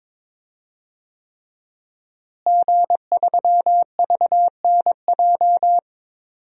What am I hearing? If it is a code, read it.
Morse code, heard as a keyed tone, about 22 wpm, 700 Hz. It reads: G3VNJ
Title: DXCC-100 Call Signs - 5 Characters - Repeated 3x - 22wpm